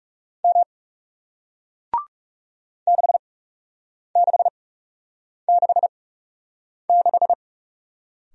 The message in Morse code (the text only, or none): M 6666